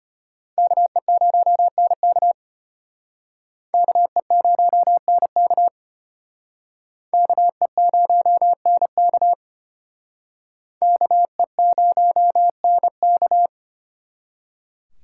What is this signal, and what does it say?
Morse code, heard as a keyed tone, about 30 wpm, 700 Hz. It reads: KE0NK KE0NK KE0NK KE0NK
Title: Speed Racing - USA 2x2 Call Signs 25wpm